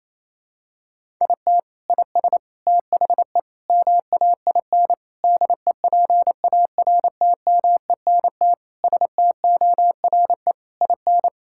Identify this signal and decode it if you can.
Morse code, heard as a keyed tone, about 28 wpm, 700 Hz. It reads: IT IS THE MAIN DEPARTMENT STORE IN
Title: Sentences from Top 1000 Words - Review Part 5 - 28wpm